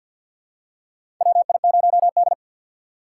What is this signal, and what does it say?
Morse code, heard as a keyed tone, about 53 wpm, 700 Hz. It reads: WI0D
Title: USA Call Signs - 4 Characters - 2x Character Spacing - 50wpm